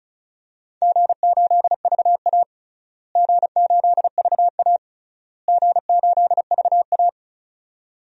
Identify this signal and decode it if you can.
Morse code, heard as a keyed tone, about 36 wpm, 700 Hz. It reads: G8VA G8VA G8VA
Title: DXCC-100 Call Signs - 4 Characters - Repeated 3x - 35wpm